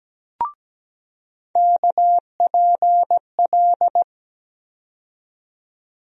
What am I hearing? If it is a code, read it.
Morse code, heard as a keyed tone, about 17 wpm, 700 Hz. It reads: KPL
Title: USA Call Sign Postfixes - 3 Letters 17wpm